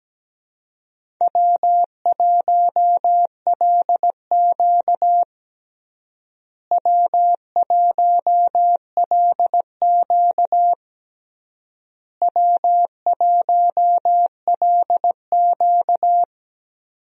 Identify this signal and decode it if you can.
Morse code, heard as a keyed tone, about 17 wpm, 700 Hz. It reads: W1LQ W1LQ W1LQ
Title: USA Call Signs - 4 Characters - Repeated 3x - 17wpm